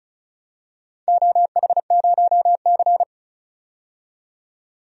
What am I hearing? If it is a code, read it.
Morse code, heard as a keyed tone, about 36 wpm, 700 Hz. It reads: OH0C
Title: DXCC-100 Call Signs - 4 Characters - 35wpm